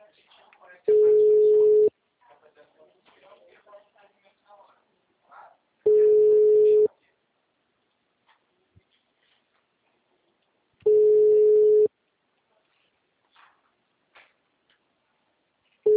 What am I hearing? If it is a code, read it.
Ringback tone (Europe, 425 Hz), with speech.
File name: out- -2013- - - .32726